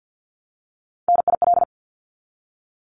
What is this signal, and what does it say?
Morse code, heard as a keyed tone, about 50 wpm, 700 Hz. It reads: NIL
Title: Abbreviations Repeated 3x - 50wpm